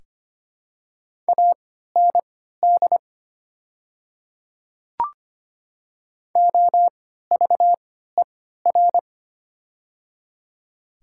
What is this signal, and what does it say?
Morse code, heard as a keyed tone, about 25 wpm, spaced as for 14 wpm, 700 Hz. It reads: AND OVER
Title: 3x Character Spacing - Top 300 Words - 25wpm